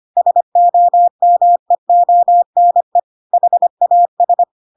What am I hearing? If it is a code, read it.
Morse code, heard as a keyed tone, about 25 wpm, 700 Hz. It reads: SOMEONE HAS
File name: Pinocchio / Pinocchio ch7 25 25 ik1hge